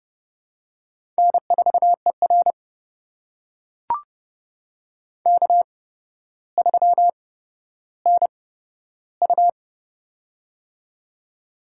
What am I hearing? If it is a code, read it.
Morse code, heard as a keyed tone, about 30 wpm, spaced as for 13 wpm, 700 Hz. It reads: N4ER K3NU
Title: USA Call Signs - 4 Characters - 8x Character Spacing - 30wpm